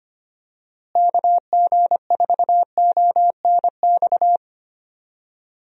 Morse code, heard as a keyed tone, about 25 wpm, 700 Hz. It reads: KG4ONX
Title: USA Call Signs - 6 Characters - Repeated 2x - 25wpm